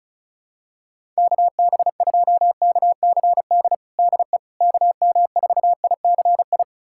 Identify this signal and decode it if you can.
Morse code, heard as a keyed tone, about 36 wpm, 700 Hz. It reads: KB2KCD DE KM4ICI
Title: USA Callsign DE Callsign 35wpm